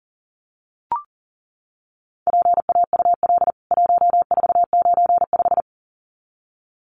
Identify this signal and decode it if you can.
Morse code, heard as a keyed tone, about 40 wpm, 700 Hz. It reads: PAUL 1495
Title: CWT Exchange 40wpm